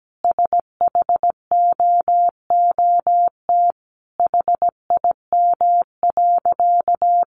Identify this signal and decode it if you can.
Morse code, heard as a keyed tone, about 17 wpm, 700 Hz. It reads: SHOOT HIM.
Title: Sentences from Top 1200 Words 17wpm